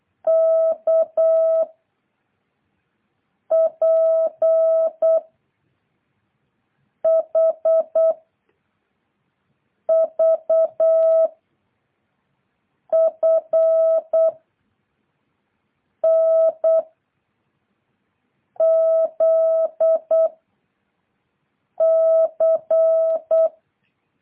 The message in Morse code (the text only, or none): K P H V F N Z C